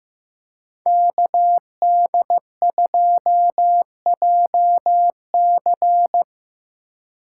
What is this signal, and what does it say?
Morse code, heard as a keyed tone, about 17 wpm, 700 Hz. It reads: KD2JC
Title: Speed Racing - USA 2x2 Call Signs 15wpm